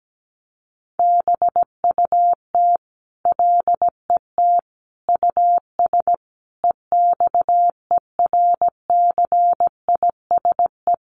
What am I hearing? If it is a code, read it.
Morse code, heard as a keyed tone, about 17 wpm, 700 Hz. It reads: BUT LET US EXERCISE
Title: Sentences from Top 1300 Words 17wpm